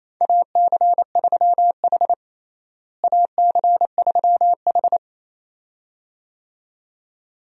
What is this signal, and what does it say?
Morse code, heard as a keyed tone, about 28 wpm, 700 Hz. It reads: AC3H AC3H
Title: USA Call Signs - 4 Characters - Repeated 2x - 28wpm